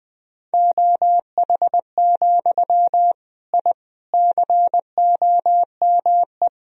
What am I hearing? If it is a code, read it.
Morse code, heard as a keyed tone, about 20 wpm, 700 Hz. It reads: OH, I COME
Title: Sentences from Top 1000 Words - Review Part 2 - 20wpm